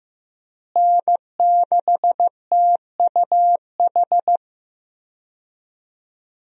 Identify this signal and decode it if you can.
Morse code, heard as a keyed tone, about 15 wpm, 700 Hz. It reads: N6TUH
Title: USA Call Signs - 5 Characters - Repeated 3x - 15wpm